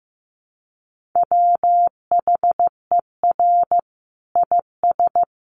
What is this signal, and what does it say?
Morse code, heard as a keyed tone, about 14 wpm, 700 Hz. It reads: WHER IS
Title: Sentences from Top 1500 Words 15wpm